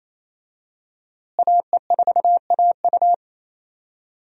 Morse code, heard as a keyed tone, about 27 wpm, 700 Hz. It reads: AE4AU
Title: USA Call Signs - 5 Characters - Repeated 2x - 28wpm